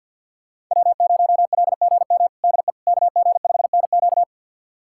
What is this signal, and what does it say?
Morse code, heard as a keyed tone, about 51 wpm, 700 Hz. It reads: W0LGM DE KG5NQ
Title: USA Callsign DE Callsign - 5 Characters - 50wpm